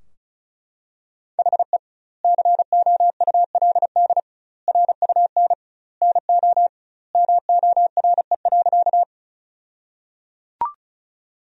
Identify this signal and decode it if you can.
Morse code, heard as a keyed tone, about 36 wpm, 700 Hz. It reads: HE COULD RUN NO MORE.
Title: 2x Word-Spacing - Sentences from Top 300 Words - 35wpm